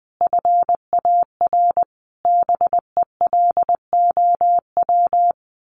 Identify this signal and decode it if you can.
Morse code, heard as a keyed tone, about 20 wpm, 700 Hz. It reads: FAR BELOW
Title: Sentences from Top 1400 Words 20wpm